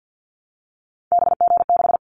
Morse code, heard as a keyed tone, about 50 wpm, 700 Hz. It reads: BG6
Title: Callsign Prefixes for Top 100 DXCC - 50wpm